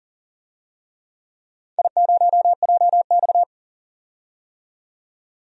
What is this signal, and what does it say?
Morse code, heard as a keyed tone, about 42 wpm, 700 Hz. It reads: I0JX
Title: DXCC-100 Call Signs - 4 Characters - 40wpm